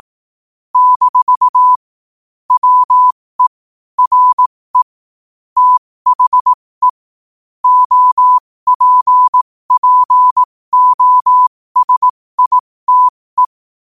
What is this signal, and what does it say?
Morse code, heard as a keyed tone, about 18 wpm, spaced as for 15 wpm, 1000 Hz. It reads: - WE RE THE OPPOSITE
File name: qotdfiles / 15wpm